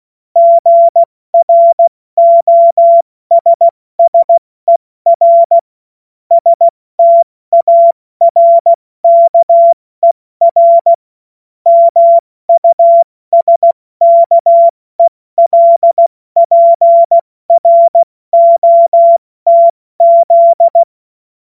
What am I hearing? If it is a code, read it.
Morse code, heard as a keyed tone, about 16 wpm, 680 Hz. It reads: GROSSER STARKER MUSKELPROTZ
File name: Morsekurs / Witze / Witz 22. Februar 2022 70 F80